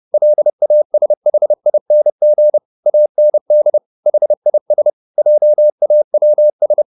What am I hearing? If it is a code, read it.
Morse code, heard as a keyed tone, about 30 wpm, 600 Hz. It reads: LASHING AND HIS JAWS